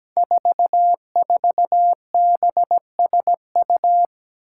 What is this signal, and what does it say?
Morse code, heard as a keyed tone, about 17 wpm, 700 Hz. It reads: 44BSU
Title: Five Challenging Letter-Number Combinations 17wpm